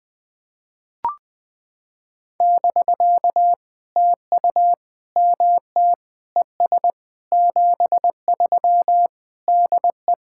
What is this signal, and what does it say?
Morse code, heard as a keyed tone, about 20 wpm, 700 Hz. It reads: <BK> TU MT ES 73 DE
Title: POTA QSO Practice - 20wpm